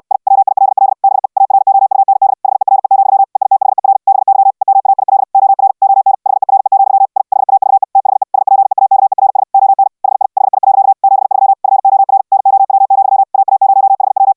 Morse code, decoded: I WILL BE AROUND HERE, EITHER BY EMAIL OR ON HERE, I HAVE THE SPEAKERS ON HI HI. 73 FOR NOW, AA0HW

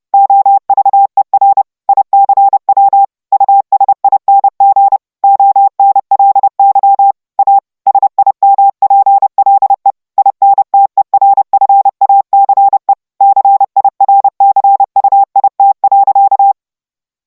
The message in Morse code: OVER ICW USING ONLY A SIMPLE INTERFACE CIRCUIT.

